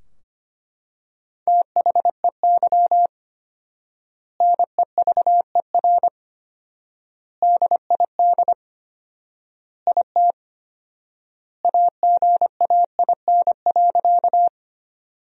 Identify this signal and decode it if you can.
Morse code, heard as a keyed tone, about 25 wpm, 700 Hz. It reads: THEY NEVER DID IT AGAIN.
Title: 4x Word-Spacing - Sentences from Top 300 Words - 25wpm